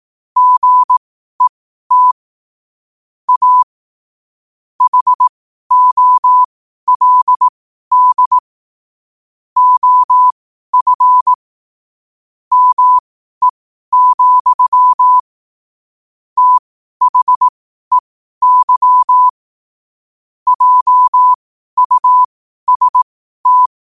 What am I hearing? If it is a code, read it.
Morse code, heard as a keyed tone, about 18 wpm, spaced as for 12 wpm, 1000 Hz. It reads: GET A HOLD OF ME, THEY JUST
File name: qotdfiles / 12wpm